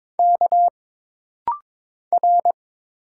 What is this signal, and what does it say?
Morse code, heard as a keyed tone, about 22 wpm, 700 Hz. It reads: K R
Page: This rapid-fire timing is geared to push your ICR (Instant Character Recognition) skills.